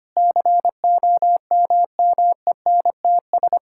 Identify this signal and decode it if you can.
Morse code, heard as a keyed tone, about 25 wpm, 700 Hz. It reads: COMMENTS
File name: practice / 25wpm / Sentences with 9-Word Phrases - 25wpm